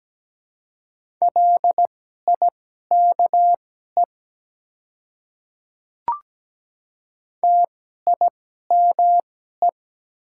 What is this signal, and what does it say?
Morse code, heard as a keyed tone, about 17 wpm, spaced as for 12 wpm, 700 Hz. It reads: LIKE TIME
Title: 2x Character Spacing - Top 100 Words - 17wpm